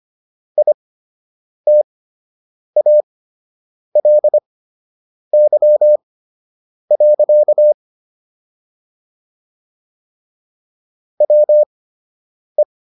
Morse code, heard as a keyed tone, about 25 wpm, spaced as for 8 wpm, 600 Hz. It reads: ITALY. WE